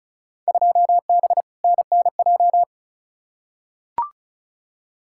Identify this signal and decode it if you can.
Morse code, heard as a keyed tone, about 34 wpm, 700 Hz. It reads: 2B NNJ
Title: Field Day Exchanges Encore - 35wpm